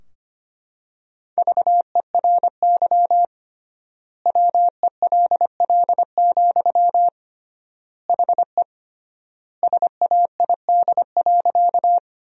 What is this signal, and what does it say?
Morse code, heard as a keyed tone, about 25 wpm, 700 Hz. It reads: VERY WELL, HE SAID.
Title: 3x Word-Spacing - Sentences from Top 200 Words - 25wpm